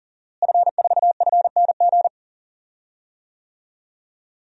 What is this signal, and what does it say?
Morse code, heard as a keyed tone, about 40 wpm, 700 Hz. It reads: F4FNG
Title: DXCC-100 Call Signs - 5 Characters - Repeated 2x - 40wpm